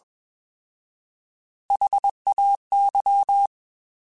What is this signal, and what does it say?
Morse code, heard as a keyed tone, about 21 wpm, 800 Hz. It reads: HAY
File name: Alphabetical / Alphabetical mp3s / S vs H @20